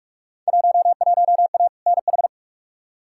Morse code, heard as a keyed tone, about 45 wpm, 700 Hz. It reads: 11A NH
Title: Field Day Exchanges Encore - 45wpm